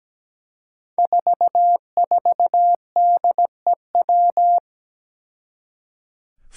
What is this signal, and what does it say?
Morse code, heard as a keyed tone, about 17 wpm, 700 Hz. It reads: 44DEW
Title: Five Challenging Letter-Number Combinations 17wpm